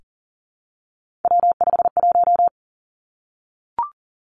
Morse code, heard as a keyed tone, about 42 wpm, 700 Hz. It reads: W51
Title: Three Challenging Letter-Number Combinations 40wpm